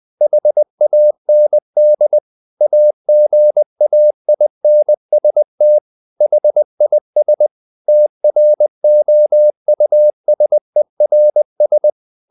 Morse code, heard as a keyed tone, about 20 wpm, 600 Hz. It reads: HAND AGAINST HIS TROUSERS
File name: the old man and the sea / the old man and the sea part 1 20wpm